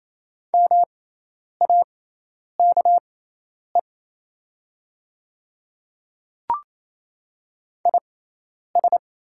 Morse code, heard as a keyed tone, about 27 wpm, spaced as for 10 wpm, 700 Hz. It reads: MAKE IS
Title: 6x Character Spacing - Top 300 Words - 28wpm